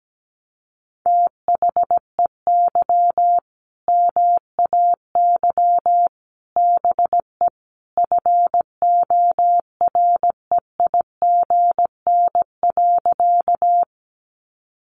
Morse code, heard as a keyed tone, about 17 wpm, 700 Hz. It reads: THEY MAY BE FOREIGN.
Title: Sentences from Top 1400 Words 17wpm